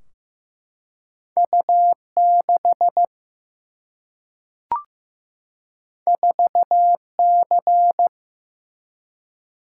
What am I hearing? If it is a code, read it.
Morse code, heard as a keyed tone, about 15 wpm, 700 Hz. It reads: U 6 4 C
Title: Single-Letter Single-Number 15wpm